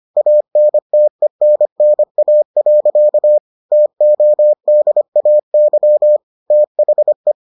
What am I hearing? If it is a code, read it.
Morse code, heard as a keyed tone, about 25 wpm, 600 Hz. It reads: ANTENNA. TODAY THE